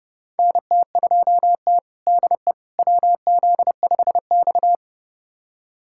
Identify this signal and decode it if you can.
Morse code, heard as a keyed tone, about 31 wpm, 700 Hz. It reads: NT2T DE WZ5X
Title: USA Callsign DE Callsign - 4 Characters - 30wpm